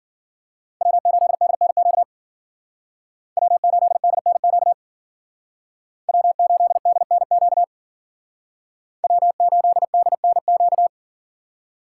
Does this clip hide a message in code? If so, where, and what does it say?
Yes: Morse code from 0.8 to 10.9 s, heard as a keyed tone, about 52 wpm, 700 Hz. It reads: W8DNQ W8DNQ W8DNQ W8DNQ